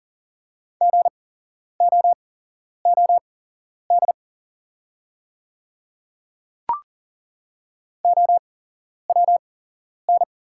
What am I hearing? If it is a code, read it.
Morse code, heard as a keyed tone, about 41 wpm, spaced as for 11 wpm, 690 Hz. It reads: GOOD OWN
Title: 8x Character Spacing - Top 300 Words - 40wpm